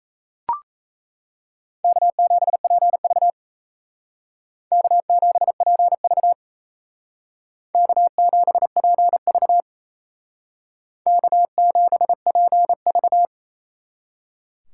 Morse code, heard as a keyed tone, about 36 wpm, 700 Hz. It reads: K7PV K7PV K7PV K7PV